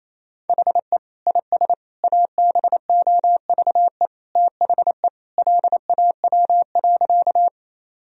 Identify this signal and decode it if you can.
Morse code, heard as a keyed tone, about 28 wpm, 700 Hz. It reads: HE IS ABOVE THE LAW.
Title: Sentences from Top 1000 Words - Review Part 3 - 28wpm